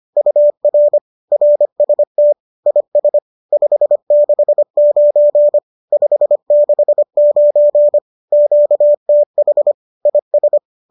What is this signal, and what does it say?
Morse code, heard as a keyed tone, about 25 wpm, 590 Hz. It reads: UR RST IS 569 569 QTH IS
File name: qso208 25